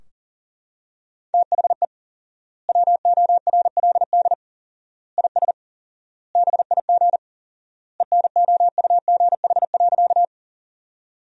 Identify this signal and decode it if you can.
Morse code, heard as a keyed tone, about 40 wpm, 700 Hz. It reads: THE WORLD IS BIG ENOUGH.
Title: 4x Word-Spacing - Sentences from Top 400 Words - 40wpm